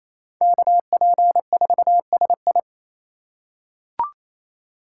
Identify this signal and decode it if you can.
Morse code, heard as a keyed tone, about 28 wpm, 700 Hz. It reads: KP4SI
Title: USA Call Signs - 3x Character Spacing - 28wpm